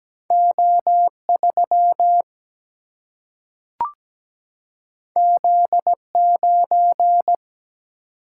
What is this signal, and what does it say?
Morse code, heard as a keyed tone, about 17 wpm, 700 Hz. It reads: O 3 Z 9
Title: Single-Letter Single-Number 17wpm